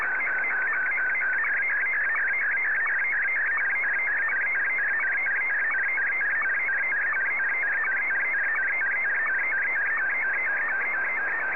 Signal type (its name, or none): SSTV (slow-scan television)